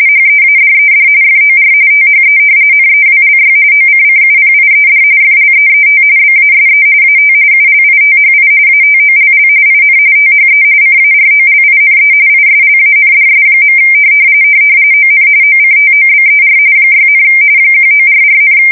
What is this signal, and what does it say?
RTTY, about 45.45 baud. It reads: HEADED FOR THE NEWLY OCCUPIED FLEET BASE AT ULITHI ATOLL IN THE CAROLINE ISLANDS. ON OCTOBER 6, 1944,
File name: fsk170 / battleshipnj